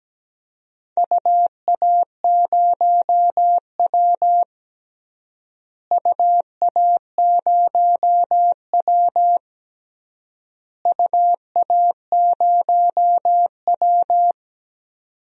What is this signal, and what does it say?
Morse code, heard as a keyed tone, about 17 wpm, 700 Hz. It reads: UA0W UA0W UA0W
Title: DXCC-100 Call Signs - 4 Characters - Repeated 3x - 17wpm